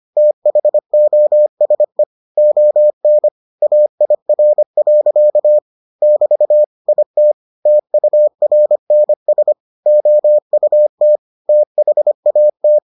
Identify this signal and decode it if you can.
Morse code, heard as a keyed tone, about 25 wpm, 600 Hz. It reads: THOSE ON AIR. <BT> IT TURNS OUT THAT